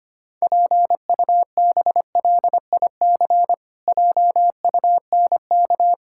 Morse code, heard as a keyed tone, about 25 wpm, 700 Hz. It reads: PUBLIC JUNK
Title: Pangram Challenge v2 - 25wpm